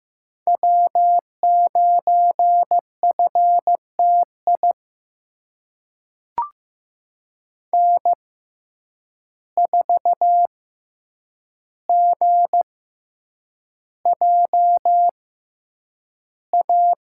Morse code, heard as a keyed tone, about 15 wpm, spaced as for 8 wpm, 700 Hz. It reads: W9FTI N4GJA
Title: USA Call Signs - 5 Characters - 6x Character Spacing - 15wpm